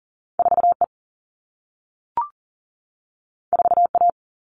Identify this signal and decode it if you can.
Morse code, heard as a keyed tone, about 42 wpm, 700 Hz. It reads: 4E 4A
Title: All Cut Numbers - 2-Digit 40wpm